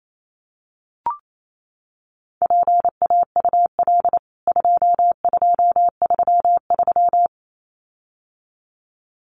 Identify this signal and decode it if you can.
Morse code, heard as a keyed tone, about 28 wpm, 700 Hz. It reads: PAUL 2233
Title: CWT Exchange 28wpm